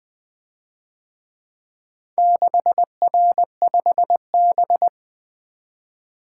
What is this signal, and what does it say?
Morse code, heard as a keyed tone, about 24 wpm, 700 Hz. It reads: 6R5B
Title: Speed-Racing - Four Letter-Number Combinations - 20wpm